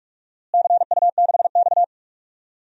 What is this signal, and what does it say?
Morse code, heard as a keyed tone, about 43 wpm, 700 Hz. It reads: CU6X
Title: DXCC-100 Call Signs - 4 Characters - Repeated 2x - 45wpm